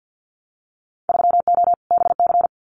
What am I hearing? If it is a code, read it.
Morse code, heard as a keyed tone, about 50 wpm, 700 Hz. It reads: 3O BC
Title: Winter Field Day Exchanges Encore 50wpm